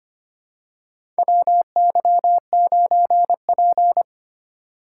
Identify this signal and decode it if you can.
Morse code, heard as a keyed tone, about 25 wpm, 700 Hz. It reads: WY9P
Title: USA Call Signs - 4 Characters - 4x Character Spacing - 25wpm